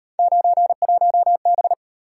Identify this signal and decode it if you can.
Morse code, heard as a keyed tone, about 34 wpm, 700 Hz. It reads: 91B